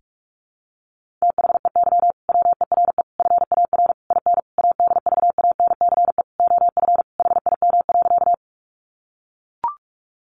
Morse code, heard as a keyed tone, about 44 wpm, 700 Hz. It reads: THEY WERE FAR IN ADVANCE OF HIM.
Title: Sentences from Top 1400 Words 45wpm